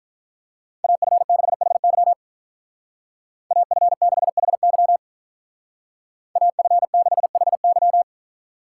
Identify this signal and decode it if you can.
Morse code, heard as a keyed tone, about 44 wpm, 700 Hz. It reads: AF6HY AF6HY AF6HY